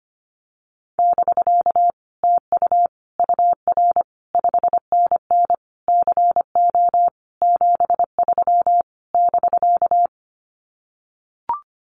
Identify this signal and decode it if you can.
Morse code, heard as a keyed tone, about 25 wpm, 700 Hz. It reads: <BK> TU UR 5NN CO 73 <BK>
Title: Common QSO Phrases Encore 25wpm